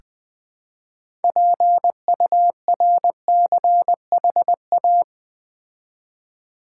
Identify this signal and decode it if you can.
Morse code, heard as a keyed tone, about 20 wpm, 700 Hz. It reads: PURCHA
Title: Build A Word - 3 Letters At A Time - 8-Letter Words - 20wpm